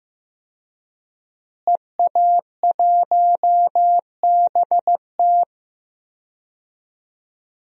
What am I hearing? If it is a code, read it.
Morse code, heard as a keyed tone, about 15 wpm, 700 Hz. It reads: EA1BT
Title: DXCC-100 Call Signs - 5 Characters - Repeated 2x - 15wpm